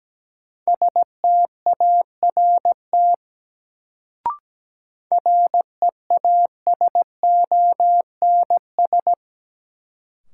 Morse code, heard as a keyed tone, about 17 wpm, 700 Hz. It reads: START REASONS
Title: Sets of 4 Words - New Word Drill - 17wpm